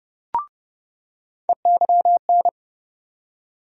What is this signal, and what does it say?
Morse code, heard as a keyed tone, about 30 wpm, 700 Hz. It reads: EYN
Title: USA Call Sign Postfixes - 3 Letters 30wpm